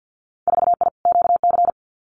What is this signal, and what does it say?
Morse code, heard as a keyed tone, about 50 wpm, 700 Hz. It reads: 4I QC
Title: Winter Field Day Exchanges Encore 50wpm